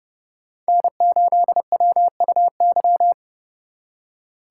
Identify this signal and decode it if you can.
Morse code, heard as a keyed tone, about 36 wpm, 700 Hz. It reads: N8WUY
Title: Speed Racing - USA Call Signs 30wpm